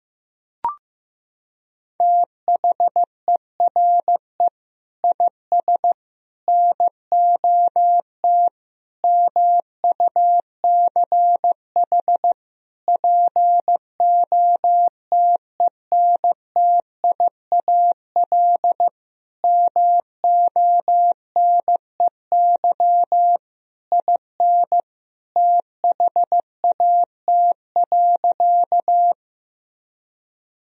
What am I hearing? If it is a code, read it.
Morse code, heard as a keyed tone, about 15 wpm, 700 Hz. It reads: THERE IS NOT MUCH POTENTIAL MONEY IN THAT.
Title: Sentences from Top 1000 Words - Review Part 4 - 15wpm